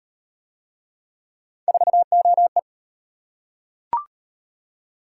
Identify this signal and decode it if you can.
Morse code, heard as a keyed tone, about 37 wpm, 700 Hz. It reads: 4OE